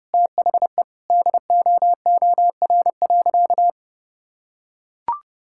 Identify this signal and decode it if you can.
Morse code, heard as a keyed tone, about 30 wpm, 700 Hz. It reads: THE DOOR.
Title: Sentences from Top 1600 Words 30wpm